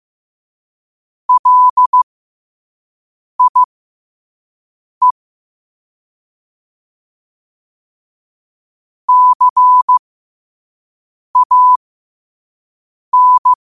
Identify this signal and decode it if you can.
Morse code, heard as a keyed tone, about 15 wpm, spaced as for 5 wpm, 1000 Hz. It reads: LIE CAN
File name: 5wpm